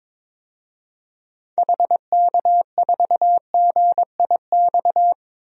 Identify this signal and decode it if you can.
Morse code, heard as a keyed tone, about 22 wpm, 700 Hz. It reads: HK4GIX
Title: DXCC-100 Call Signs - 6 Characters - 22wpm